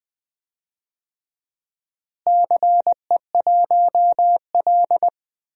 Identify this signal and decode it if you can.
Morse code, heard as a keyed tone, about 20 wpm, 700 Hz. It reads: CE1L
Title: DXCC-100 Call Signs - 4 Characters - Repeated 2x - 20wpm